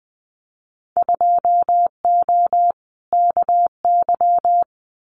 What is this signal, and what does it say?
Morse code, heard as a keyed tone, about 20 wpm, 700 Hz. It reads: 2O KY
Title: Winter Field Day Exchanges 20wpm